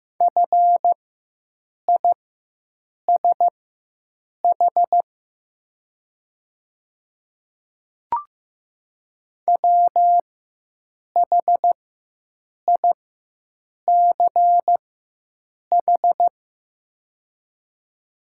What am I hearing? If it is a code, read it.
Morse code, heard as a keyed tone, about 15 wpm, spaced as for 7 wpm, 700 Hz. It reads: FISH WHICH
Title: 4x Character Spacing - Top 300 Words - 15wpm